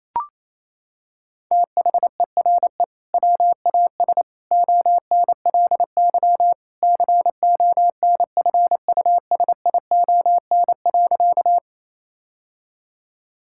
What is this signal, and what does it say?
Morse code, heard as a keyed tone, about 27 wpm, 700 Hz. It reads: THERE WAS ONLY CONFUSION.
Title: Sentences from Top 1600 Words 28wpm